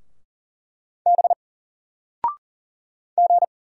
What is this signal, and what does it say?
Morse code, heard as a keyed tone, about 40 wpm, 700 Hz. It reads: B G
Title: Single Letter-Number 40wpm